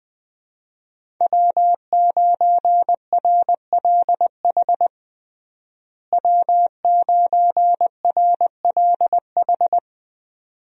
Morse code, heard as a keyed tone, about 20 wpm, 700 Hz. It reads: W9RLH W9RLH
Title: USA Call Signs - 5 Characters - Repeated 2x - 20wpm